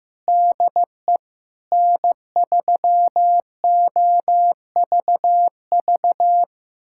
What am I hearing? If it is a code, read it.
Morse code, heard as a keyed tone, about 15 wpm, 700 Hz. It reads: DE N3OVV
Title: USA Callsign DE Callsign - 5 Characters - 15wpm